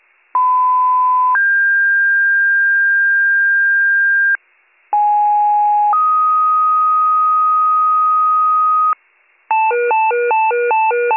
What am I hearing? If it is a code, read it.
Two-tone page, heard as a one-second tone, then a three-second tone, about 1010 then 1600 Hz, about 830 then 1250 Hz, with noise. It is heard from 0.3 to 8.9 s.